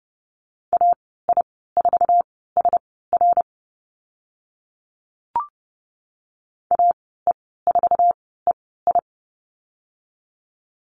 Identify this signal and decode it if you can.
Morse code, heard as a keyed tone, about 30 wpm, spaced as for 17 wpm, 700 Hz. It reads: AI4SR AE4EI
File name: Lesson 14g - Number 2 - Call Signs 30wpm